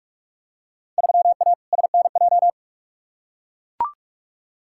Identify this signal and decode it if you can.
Morse code, heard as a keyed tone, about 41 wpm, 700 Hz. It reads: 3A SNJ
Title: Field Day Exchanges - 45wpm